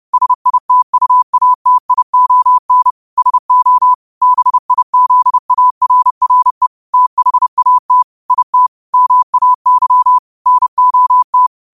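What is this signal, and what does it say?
Morse code, heard as a keyed tone, about 30 wpm, 1000 Hz. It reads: SITUATION SO BIZARRE THAT IT MAY NOT
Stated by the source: Morse quote of the day at 30 WPM.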